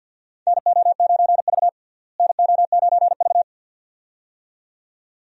Morse code, heard as a keyed tone, about 48 wpm, 700 Hz. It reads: NO9V NO9V
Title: USA Call Signs - 4 Characters - Repeated 2x - 50wpm